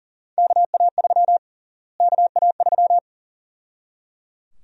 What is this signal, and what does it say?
Morse code, heard as a keyed tone, about 37 wpm, 700 Hz. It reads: KA3 KA3
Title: USA Call Sign Prefixes - Repeated 2x - 40wpm